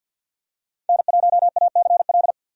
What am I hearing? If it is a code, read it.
Morse code, heard as a keyed tone, about 52 wpm, 700 Hz. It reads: N1ACL
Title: USA Call Signs - 5 Characters - 50wpm